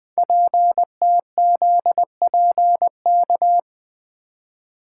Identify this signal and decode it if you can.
Morse code, heard as a keyed tone, about 20 wpm, 700 Hz. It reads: PTZPK
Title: Code Groups - 5 Letters - 20wpm